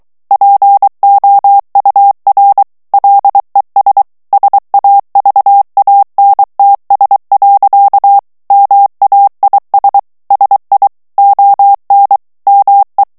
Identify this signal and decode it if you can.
Morse code, heard as a keyed tone, about 23 wpm, 800 Hz. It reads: POUR LES SAVANTS. MAIS SI ON ME
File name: FRANCAIS TEXTE CLAIR 2 VITESSE 22 MOTS MINUTES